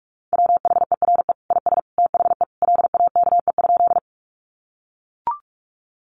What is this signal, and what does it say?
Morse code, heard as a keyed tone, about 44 wpm, 700 Hz. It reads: WHERE IS THE LAKE?
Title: Sentences from Top 1100 Words 45wpm